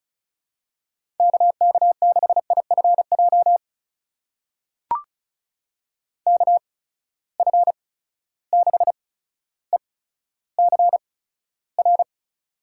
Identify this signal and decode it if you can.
Morse code, heard as a keyed tone, about 36 wpm, spaced as for 15 wpm, 700 Hz. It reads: KK6IFJ KF6ECR
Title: USA Call Signs - 6 Characters - 8x Character Spacing - 35wpm